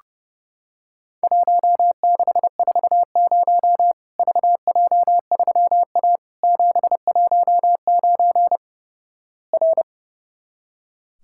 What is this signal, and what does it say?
Morse code, heard as a keyed tone, about 32 wpm, 700 Hz. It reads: 1640 VJ3A 719 R
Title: Worked All Europe DX Contest - QTC Exchange - 30wpm